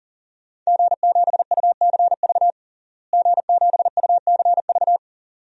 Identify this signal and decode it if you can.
Morse code, heard as a keyed tone, about 40 wpm, 700 Hz. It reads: G7UCV G7UCV
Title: DXCC-100 Call Signs - 5 Characters - Repeated 2x - 40wpm